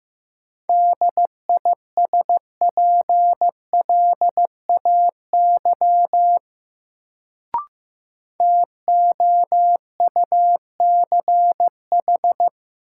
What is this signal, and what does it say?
Morse code, heard as a keyed tone, about 15 wpm, 700 Hz. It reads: DISPLAY TOUCH
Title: Sentences from Top 1200 Words - New Word Drill Alt - 15wpm